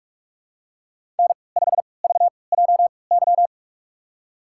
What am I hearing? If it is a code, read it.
Morse code, heard as a keyed tone, about 41 wpm, spaced as for 25 wpm, 700 Hz. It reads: N5VJY
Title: USA Call Signs - 5 Characters - 3x Character Spacing - 45wpm